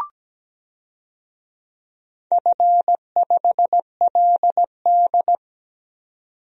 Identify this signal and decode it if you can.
Morse code, heard as a keyed tone, about 17 wpm, 700 Hz. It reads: F5LD
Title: DXCC-100 Call Signs - 4 Characters - 17wpm